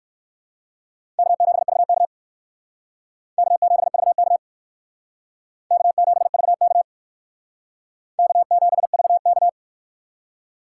Morse code, heard as a keyed tone, about 60 wpm, 700 Hz. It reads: K7VK K7VK K7VK K7VK